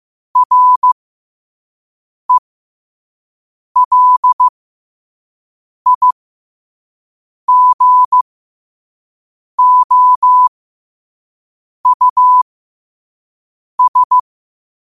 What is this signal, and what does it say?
Morse code, heard as a keyed tone, about 15 wpm, spaced as for 5 wpm, 1000 Hz. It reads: RELIGOUS
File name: qotdfiles / 5wpm